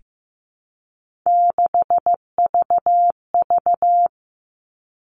Morse code, heard as a keyed tone, about 15 wpm, 700 Hz. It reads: 6VV
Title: All Cut Numbers - 3-Digit 15wpm